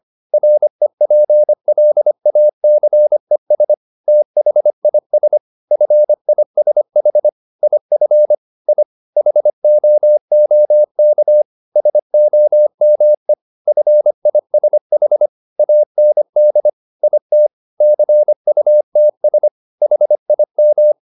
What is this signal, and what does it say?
Morse code, heard as a keyed tone, about 25 wpm, 600 Hz. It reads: REPLACES THIS FISH IF I HOOK SOME FISH AND IT CUTS HIM
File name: the old man and the sea part 1 25wpm